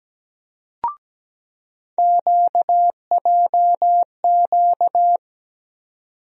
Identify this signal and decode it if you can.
Morse code, heard as a keyed tone, about 17 wpm, 700 Hz. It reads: QJQ
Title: USA Call Sign Postfixes - 3 Letters 17wpm